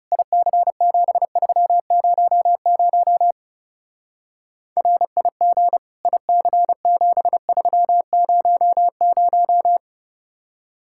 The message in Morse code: IC7300 RIG IC7300

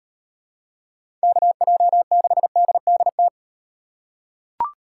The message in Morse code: KJ6DDT

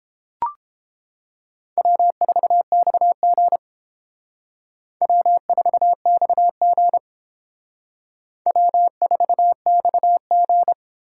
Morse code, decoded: W4XG W4XG W4XG

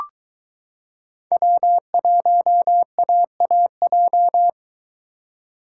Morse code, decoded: W1AAJ

